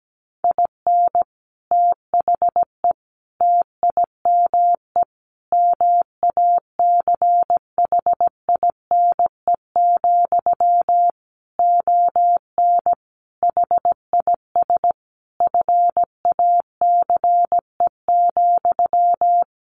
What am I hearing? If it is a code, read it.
Morse code, heard as a keyed tone, about 17 wpm, 700 Hz. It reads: IN THE TIME MACHINE, ON HIS FACE,
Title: A Sound Of Thunder 17wpm